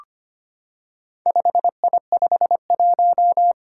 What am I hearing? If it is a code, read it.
Morse code, heard as a keyed tone, about 25 wpm, 700 Hz. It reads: 5I51
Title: Four Challenging Letter-Number Combinations 25wpm